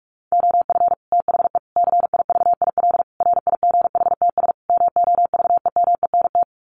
Morse code, heard as a keyed tone, about 44 wpm, 700 Hz. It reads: OF THE CIVIL RIGHTS MOVEMENT
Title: Sets of 5 Words Encore2 45wpm